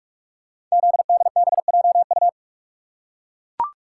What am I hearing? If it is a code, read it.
Morse code, heard as a keyed tone, about 44 wpm, 700 Hz. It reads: ZDBJU
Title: Code Groups - 5 Letters - 45wpm